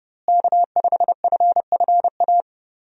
Morse code, heard as a keyed tone, about 32 wpm, 700 Hz. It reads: K5FFA
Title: USA Call Signs - 5 Characters - Repeated 3x - 30wpm